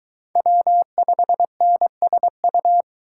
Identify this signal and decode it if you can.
Morse code, heard as a keyed tone, about 23 wpm, 700 Hz. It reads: W5NSU